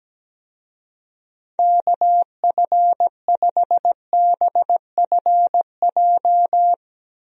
Morse code, heard as a keyed tone, about 17 wpm, 700 Hz. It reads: KF5BFJ
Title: USA Call Signs - 6 Characters - Repeated 3x - 17wpm